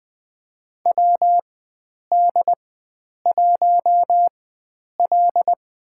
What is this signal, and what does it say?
Morse code, heard as a keyed tone, about 20 wpm, spaced as for 9 wpm, 700 Hz. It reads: WD1L
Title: USA Call Signs - 4x Character Spacing - 20wpm